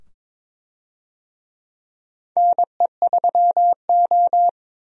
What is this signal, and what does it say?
Morse code, heard as a keyed tone, about 23 wpm, 700 Hz. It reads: NE3O
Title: USA Call Signs - 4 Characters - 22wpm